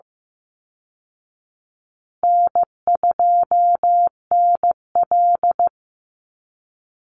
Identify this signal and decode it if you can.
Morse code, heard as a keyed tone, about 15 wpm, 700 Hz. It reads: N2NL
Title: Easy Call Signs Repeated 2x - 15wpm